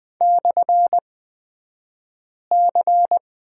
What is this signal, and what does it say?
Morse code, heard as a keyed tone, about 20 wpm, spaced as for 5 wpm, 700 Hz. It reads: /C
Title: 249 Lesson 40d - BK - Two Characters 20wpm